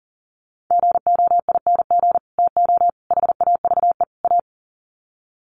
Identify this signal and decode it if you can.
Morse code, heard as a keyed tone, about 40 wpm, 700 Hz. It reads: GOING TO HAVE A
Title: Sets of 4 Words 40wpm